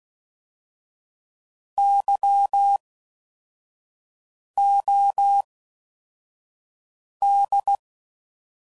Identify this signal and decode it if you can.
Morse code, heard as a keyed tone, about 16 wpm, spaced as for 4 wpm, 800 Hz. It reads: YOD